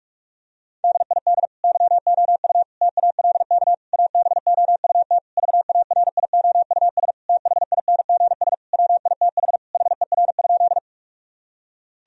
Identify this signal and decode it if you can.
Morse code, heard as a keyed tone, about 46 wpm, 700 Hz. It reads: DID YOU TALK ABOUT VARIOUS THINGS WITH HER?
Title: Sentences from Top 1000 Words - Review Part 4 - 45wpm